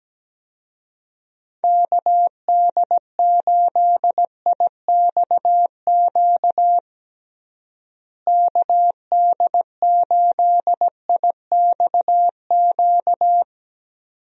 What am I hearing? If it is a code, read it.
Morse code, heard as a keyed tone, about 17 wpm, 700 Hz. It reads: KD8IXQ KD8IXQ
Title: USA Call Signs - 6 Characters - Repeated 2x - 17wpm